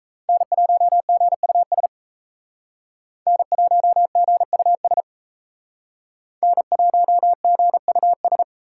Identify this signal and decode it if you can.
Morse code, heard as a keyed tone, about 36 wpm, 700 Hz. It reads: N1GUS N1GUS N1GUS